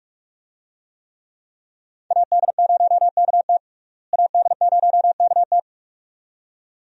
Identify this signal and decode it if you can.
Morse code, heard as a keyed tone, about 46 wpm, 700 Hz. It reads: AD0KT AD0KT
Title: USA Call Signs - 5 Characters - Repeated 2x - 45wpm